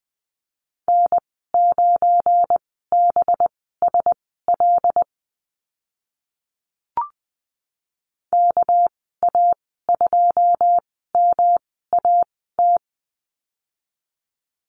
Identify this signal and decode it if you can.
Morse code, heard as a keyed tone, about 20 wpm, spaced as for 14 wpm, 700 Hz. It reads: N9BSL KA2MAT
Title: 200 Lesson 32g - Letter k - Call Signs 20wpm